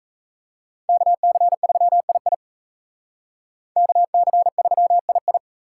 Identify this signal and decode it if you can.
Morse code, heard as a keyed tone, about 36 wpm, 700 Hz. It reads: KC3II KC3II